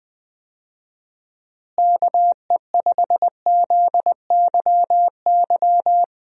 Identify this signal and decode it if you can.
Morse code, heard as a keyed tone, about 20 wpm, 700 Hz. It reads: KE5ZYY
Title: USA Call Signs - 6 Characters - Repeated 2x - 20wpm